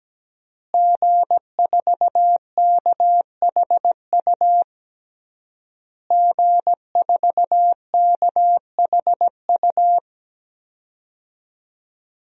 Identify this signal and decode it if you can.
Morse code, heard as a keyed tone, about 17 wpm, 700 Hz. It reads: G4KHU G4KHU
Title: DXCC-100 Call Signs - 5 Characters - Repeated 3x - 17wpm